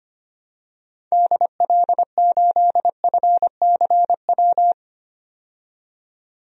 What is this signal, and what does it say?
Morse code, heard as a keyed tone, about 25 wpm, 700 Hz. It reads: DL8FCW
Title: DXCC-100 Call Signs - 6 Characters - 25wpm